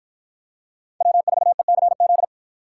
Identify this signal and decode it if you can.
Morse code, heard as a keyed tone, about 49 wpm, 700 Hz. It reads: W4ECZ